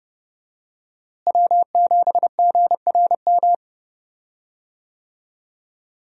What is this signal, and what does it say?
Morse code, heard as a keyed tone, about 31 wpm, 700 Hz. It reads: W7GRM
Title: USA Call Signs - 5 Characters - Repeated 2x - 30wpm